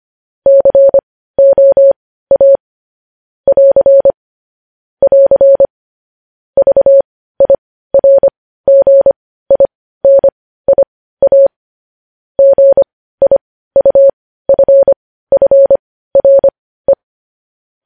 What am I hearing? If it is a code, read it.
Morse code, heard as a keyed tone, about 25 wpm, spaced as for 15 wpm, 550 Hz. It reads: COA <AR> <AR> VIRGINIA GIUFFRE
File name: podcast / News headlines-1x-25x15